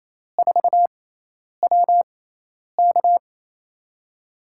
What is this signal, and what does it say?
Morse code, heard as a keyed tone, about 28 wpm, spaced as for 10 wpm, 700 Hz. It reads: 4WK
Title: USA Call Signs - 5 Characters - 6x Character Spacing - 28wpm